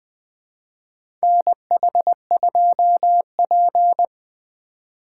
Morse code, heard as a keyed tone, about 20 wpm, 700 Hz. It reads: NH2P
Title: USA Call Signs - 4 Characters - 4x Character Spacing - 20wpm